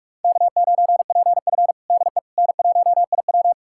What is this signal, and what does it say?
Morse code, heard as a keyed tone, about 42 wpm, 700 Hz. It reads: K9PF DE N1IW
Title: USA Callsign DE Callsign - 4 Characters - 45wpm